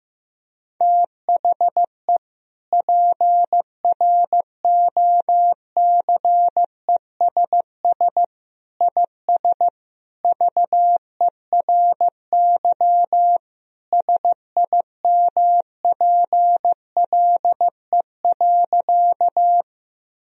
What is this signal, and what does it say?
Morse code, heard as a keyed tone, about 15 wpm, 700 Hz. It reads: THE PROCESS IS VERY SIMPLE.
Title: Sentences from Top 1000 Words - Review Part 3 - 15wpm